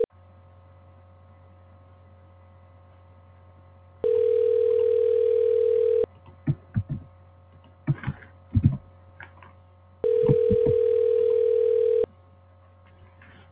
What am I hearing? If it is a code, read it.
Ringback tone (North America), 440 + 480 Hz.
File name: recordings / RE902d09cc5b1803c3d559de2c6a0eab90